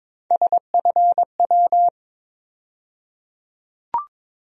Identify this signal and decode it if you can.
Morse code, heard as a keyed tone, about 22 wpm, 700 Hz. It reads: SFW
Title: USA Call Sign Postfixes - 3 Letters 22wpm